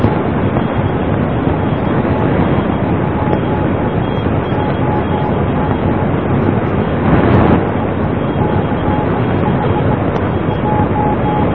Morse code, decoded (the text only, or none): none